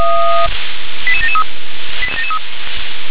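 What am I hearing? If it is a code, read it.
Five-tone selective call (ZVEI), heard as quick runs of five tones, tones of about 70 ms. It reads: 88773
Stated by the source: ZVEI five-tone selective call received by SDR, code 88790.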